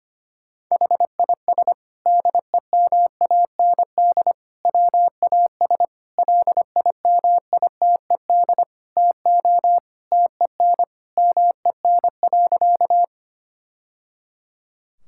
Morse code, heard as a keyed tone, about 25 wpm, 700 Hz. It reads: HIS DEMAND WAS LIMITED TO TEN MEN.
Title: Sentences from Top 1000 Words - Review Part 6 - 25wpm